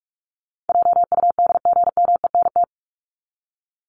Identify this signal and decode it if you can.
Morse code, heard as a keyed tone, about 44 wpm, 700 Hz. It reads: JUDGMENT
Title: Top 1751-2000 Words 45wpm